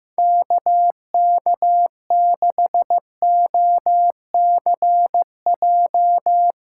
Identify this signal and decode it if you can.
Morse code, heard as a keyed tone, about 16 wpm, 700 Hz. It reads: KK6OCJ
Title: Speed Racing - USA Call Signs 15wpm